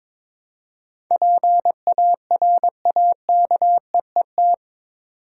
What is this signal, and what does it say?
Morse code, heard as a keyed tone, about 22 wpm, 700 Hz. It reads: PARAKEET
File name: Difficult Words with Hints - 22wpm